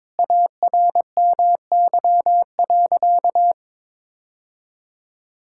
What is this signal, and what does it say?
Morse code, heard as a keyed tone, about 22 wpm, 700 Hz. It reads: ARMY.
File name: practice / 22wpm / Sentences with 3-Word Phrases - 22wpm